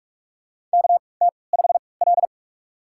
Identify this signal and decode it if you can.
Morse code, heard as a keyed tone, about 46 wpm, spaced as for 26 wpm, 700 Hz. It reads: KT5L
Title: USA Call Signs - 3x Character Spacing - 45wpm